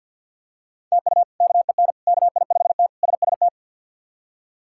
Morse code, heard as a keyed tone, about 48 wpm, 700 Hz. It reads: TU KEN KI5T SST
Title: K1USN SST QSO Practice - 50wpm